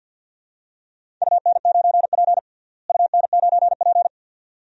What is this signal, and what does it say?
Morse code, heard as a keyed tone, about 52 wpm, 700 Hz. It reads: UN9P UN9P
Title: DXCC-100 Call Signs - 4 Characters - Repeated 2x - 50wpm